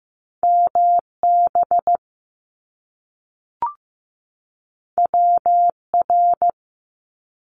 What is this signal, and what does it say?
Morse code, heard as a keyed tone, about 15 wpm, 700 Hz. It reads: MB WR
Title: Top Two Letter Combinations in Words 15wpm